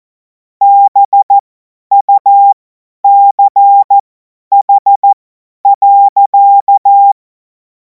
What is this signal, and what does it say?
Morse code, heard as a keyed tone, about 14 wpm, spaced as for 10 wpm, 800 Hz. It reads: BUCH.
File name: Morsekurs / Witze / Witz 03. Februar 2026 50 F70